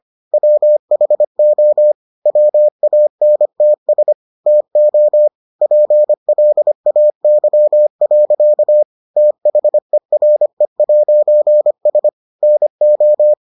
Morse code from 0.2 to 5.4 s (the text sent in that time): WHO WANTS TO